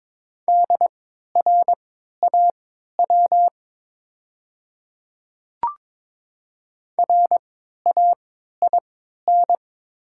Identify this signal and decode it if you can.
Morse code, heard as a keyed tone, about 22 wpm, spaced as for 12 wpm, 700 Hz. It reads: DRAW RAIN
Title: 3x Character Spacing - Top 300 Words - 22wpm